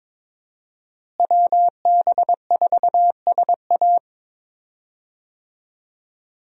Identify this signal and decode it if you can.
Morse code, heard as a keyed tone, about 22 wpm, 700 Hz. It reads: WB4SA
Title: USA Call Signs - 5 Characters - Repeated 2x - 22wpm